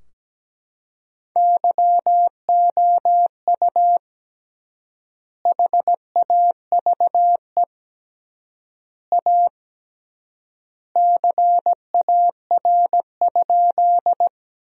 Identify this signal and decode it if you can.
Morse code, heard as a keyed tone, about 17 wpm, 700 Hz. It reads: YOU HAVE A CAR?
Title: 3x Word-Spacing - Sentences from Top 300 Words - 17wpm